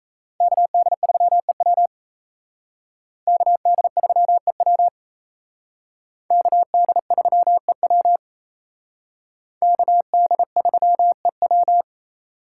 Morse code, heard as a keyed tone, about 35 wpm, 700 Hz. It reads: KD3EW KD3EW KD3EW KD3EW